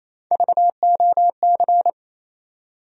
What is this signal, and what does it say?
Morse code, heard as a keyed tone, about 28 wpm, 700 Hz. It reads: VOC
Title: USA Call Sign Postfixes - 3 Letters 28wpm